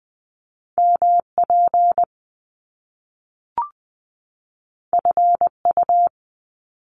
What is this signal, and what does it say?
Morse code, heard as a keyed tone, about 20 wpm, 700 Hz. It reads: MP FU
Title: Top Two Letter Combinations in Words 20wpm